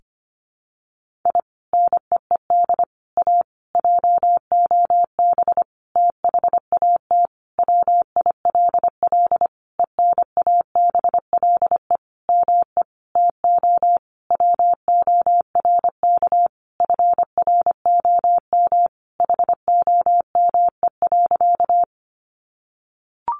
Morse code, heard as a keyed tone, about 25 wpm, 700 Hz. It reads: I NEED A JOB THAT WILL ENABLE ME TO WORK FROM HOME.
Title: Sentences from Top 1500 Words 25wpm